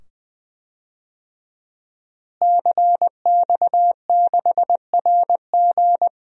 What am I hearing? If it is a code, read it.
Morse code, heard as a keyed tone, about 20 wpm, 700 Hz. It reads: CX6RG
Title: DXCC-100 Call Signs - 5 Characters - Repeated 2x - 20wpm